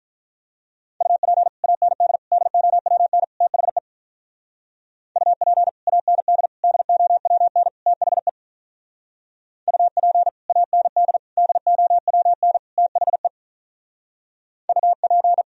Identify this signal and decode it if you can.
Morse code, heard as a keyed tone, about 46 wpm, 700 Hz. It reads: UP AND DOWN THE UP AND DOWN THE UP AND DOWN THE UP